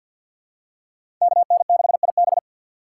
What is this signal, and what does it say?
Morse code, heard as a keyed tone, about 47 wpm, 690 Hz. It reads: KN6IB
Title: USA Call Signs - 5 Characters - 3x Character Spacing - 50wpm